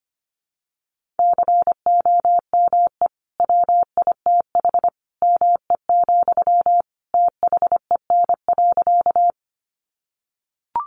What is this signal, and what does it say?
Morse code, heard as a keyed tone, about 25 wpm, 700 Hz. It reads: COME WITH ME, THEN.
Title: Sentences from Top 200 Words 25wpm